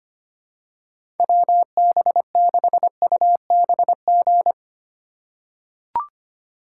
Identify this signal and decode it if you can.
Morse code, heard as a keyed tone, about 25 wpm, 700 Hz. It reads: WB6UBG
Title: USA Call Signs - 6 Characters - 2x Character Spacing - 25wpm